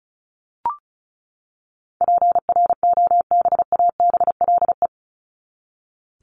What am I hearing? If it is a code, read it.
Morse code, heard as a keyed tone, about 34 wpm, 700 Hz. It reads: PROBABLE
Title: LY Suffix - Top 100 Words with Base Words 35wpm